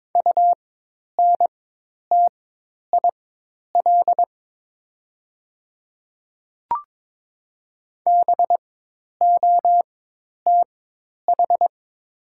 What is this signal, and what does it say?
Morse code, heard as a keyed tone, about 22 wpm, spaced as for 10 wpm, 700 Hz. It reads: UNTIL BOTH
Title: 4x Character Spacing - Top 300 Words - 22wpm